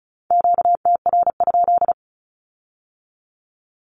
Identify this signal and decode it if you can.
Morse code, heard as a keyed tone, about 36 wpm, 700 Hz. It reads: QTR?
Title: Q-Codes with Spoken Meaning - 35wpm